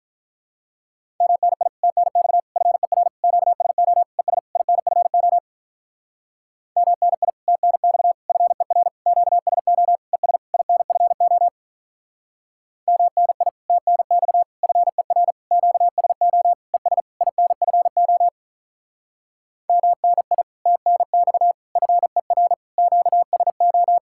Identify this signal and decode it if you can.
Morse code, heard as a keyed tone, about 44 wpm, 700 Hz. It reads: MNI TNX FER QSO ES INFO MNI TNX FER QSO ES INFO MNI TNX FER QSO ES INFO MNI TNX FER QSO